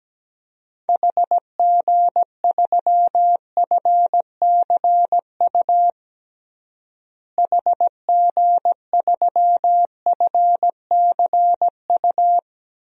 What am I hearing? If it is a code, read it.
Morse code, heard as a keyed tone, about 17 wpm, 700 Hz. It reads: HG3FCU HG3FCU
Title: DXCC-100 Call Signs - 6 Characters - Repeated 3x - 17wpm